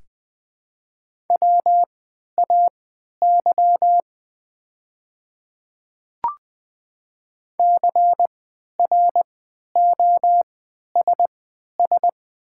Morse code, heard as a keyed tone, about 20 wpm, spaced as for 11 wpm, 700 Hz. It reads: WAY CROSS
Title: 3x Character Spacing - Top 300 Words - 20wpm